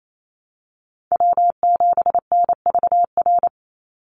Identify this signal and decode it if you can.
Morse code, heard as a keyed tone, about 28 wpm, 700 Hz. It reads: W7NVR
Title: USA Call Signs 28wpm